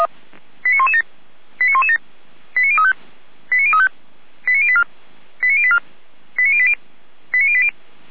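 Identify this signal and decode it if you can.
Five-tone selective call (ZVEI), heard as quick runs of five tones, tones of about 70 ms. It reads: 89117 89117 89035 89035 89084 89084 89080 89080
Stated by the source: ZVEI five-tone selective call received on SDR.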